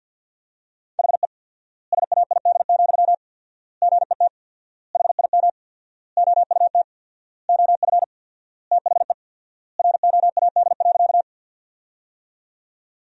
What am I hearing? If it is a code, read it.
Morse code, heard as a keyed tone, about 50 wpm, 700 Hz. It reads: HE SAID, GET HIM OUT OF THE ROAD.
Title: 4x Word-Spacing - Sentences from Top 400 Words - 50wpm